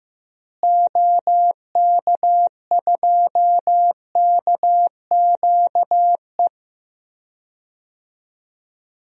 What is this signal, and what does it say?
Morse code, heard as a keyed tone, about 15 wpm, 700 Hz. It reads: OK2KQE
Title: DXCC-100 Call Signs - 6 Characters - Repeated 2x - 15wpm